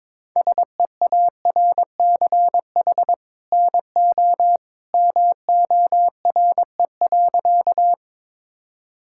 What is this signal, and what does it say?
Morse code, heard as a keyed tone, about 22 wpm, 700 Hz. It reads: SEARCH NO MORE.
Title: Sentences from Top 1000 Words - Review Part 5 - 22wpm